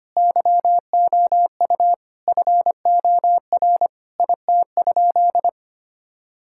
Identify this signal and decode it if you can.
Morse code, heard as a keyed tone, about 25 wpm, 700 Hz. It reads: YOU FOR IT?
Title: Sentences from Top 1000 Words - Review Part 6 - 25wpm